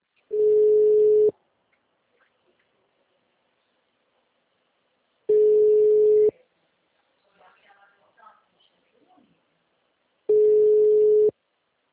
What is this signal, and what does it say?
Ringback tone (Europe, 425 Hz), with speech.